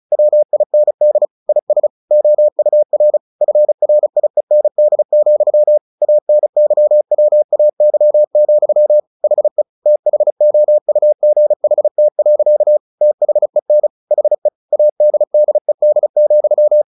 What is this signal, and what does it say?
Morse code, heard as a keyed tone, about 34 wpm, 600 Hz. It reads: WIND IS OUR FRIEND, ANYWAY, HE THOUGHT. THEN HE ADDED,
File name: the old man and the sea 2 35wpm